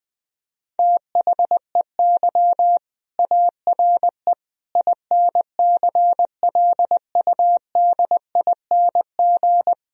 Morse code, heard as a keyed tone, about 20 wpm, 700 Hz. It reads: THEY ARE INCLUDING
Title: Sentences from Top 1000 Words - Review Part 4 - 20wpm